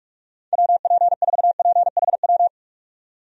Morse code, heard as a keyed tone, about 45 wpm, 700 Hz. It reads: WP4PHW
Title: USA Call Signs - 6 Characters - 45wpm